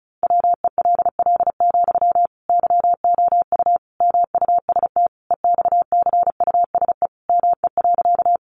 Morse code, heard as a keyed tone, about 34 wpm, 700 Hz. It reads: WELL, YOU MUST EXCUSE ME.
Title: Sentences from Top 700 Words 35wpm